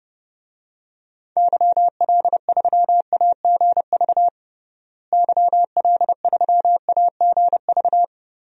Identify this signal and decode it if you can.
Morse code, heard as a keyed tone, about 30 wpm, 700 Hz. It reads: YL3AGV YL3AGV
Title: DXCC-100 Call Signs - 6 Characters - Repeated 2x - 30wpm